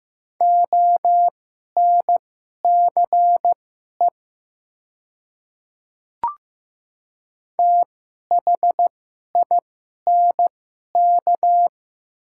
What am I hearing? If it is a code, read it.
Morse code, heard as a keyed tone, about 15 wpm, spaced as for 11 wpm, 690 Hz. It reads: ONCE THINK
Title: 2x Character Spacing - Top 300 Words - 15wpm